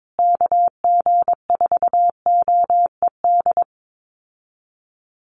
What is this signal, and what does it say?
Morse code, heard as a keyed tone, about 22 wpm, 700 Hz. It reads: KG4OED
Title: USA Call Signs Repeated 3x 22wpm